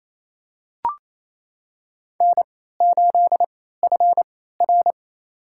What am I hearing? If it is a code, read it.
Morse code, heard as a keyed tone, about 28 wpm, spaced as for 16 wpm, 700 Hz. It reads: N8FR
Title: USA Call Signs - 4 Characters - 3x Character Spacing - 28wpm